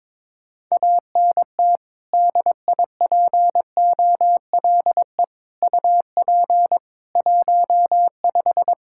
Morse code, heard as a keyed tone, about 23 wpm, 700 Hz. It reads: ANT DIPOLE UP 15
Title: Practice QSO - Exchange 2 - Info - 22wpm